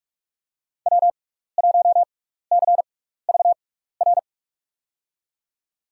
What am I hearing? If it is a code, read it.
Morse code, heard as a keyed tone, about 46 wpm, spaced as for 15 wpm, 700 Hz. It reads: W1CVR
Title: USA Call Signs - 6x Character Spacing - 45wpm